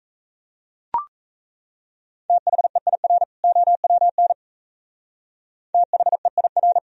Morse code, heard as a keyed tone, about 34 wpm, 690 Hz. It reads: THEIR OWN THEIR
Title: Speed-Racing - Sets of 2 Words 28wpm